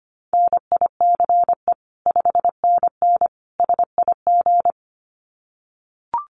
Morse code, heard as a keyed tone, about 25 wpm, 700 Hz. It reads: NICE 5NN SIG
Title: Common QSO Phrases 25wpm